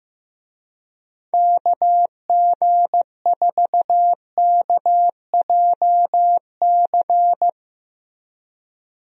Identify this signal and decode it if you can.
Morse code, heard as a keyed tone, about 15 wpm, 700 Hz. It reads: KG4KJC
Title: USA Call Signs - 6 Characters - Repeated 2x - 15wpm